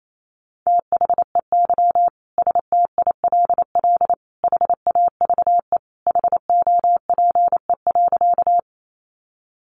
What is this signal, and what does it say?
Morse code, heard as a keyed tone, about 28 wpm, 700 Hz. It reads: THEY STILL HAVE HOPE.
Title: Sentences from Top 1000 Words 28wpm